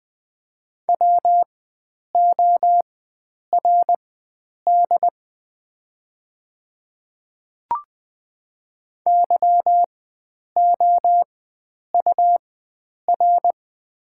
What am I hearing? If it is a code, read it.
Morse code, heard as a keyed tone, about 20 wpm, spaced as for 9 wpm, 690 Hz. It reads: WORD YOUR
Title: 4x Character Spacing - Top 100 Words - 20wpm